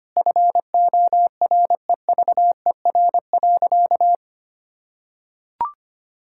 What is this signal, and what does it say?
Morse code, heard as a keyed tone, about 25 wpm, 700 Hz. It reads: FOREVER.
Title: Sentences from Top 1600 Words 25wpm